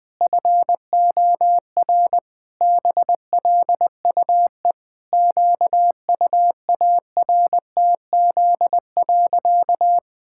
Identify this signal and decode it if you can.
Morse code, heard as a keyed tone, about 21 wpm, 700 Hz. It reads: FOR BLUE QUARTZ.
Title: Pangram Challenge v2 - 20wpm